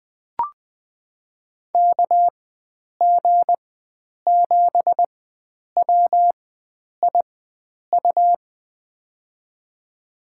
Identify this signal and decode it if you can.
Morse code, heard as a keyed tone, about 20 wpm, spaced as for 9 wpm, 700 Hz. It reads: KG7WIU
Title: USA Call Signs - 6 Characters - 4x Character Spacing - 20wpm